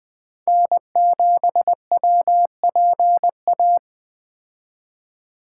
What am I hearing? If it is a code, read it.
Morse code, heard as a keyed tone, about 20 wpm, 700 Hz. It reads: N7WPA
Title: Speed Racing - USA Call Signs 17wpm